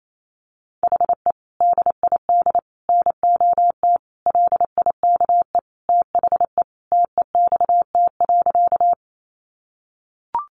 Morse code, heard as a keyed tone, about 28 wpm, 700 Hz. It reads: HE DID NOT LIKE THE TEXT.
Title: Sentences from Top 1200 Words 28wpm